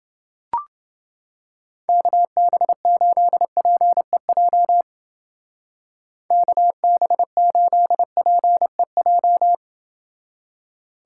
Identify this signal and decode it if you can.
Morse code, heard as a keyed tone, about 28 wpm, 700 Hz. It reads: KB8PEJ KB8PEJ